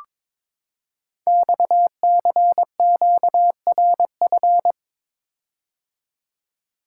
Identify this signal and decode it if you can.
Morse code, heard as a keyed tone, about 22 wpm, 700 Hz. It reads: XCQRF
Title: Code Groups - 5 Letters - 22wpm